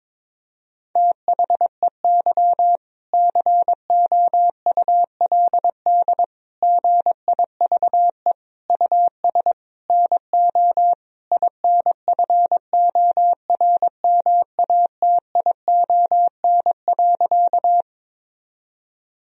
Morse code, heard as a keyed tone, about 22 wpm, 700 Hz. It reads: THEY COULD GIVE US NO INFORMATION.
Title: Sentences from Top 1000 Words - Review Part 3 - 22wpm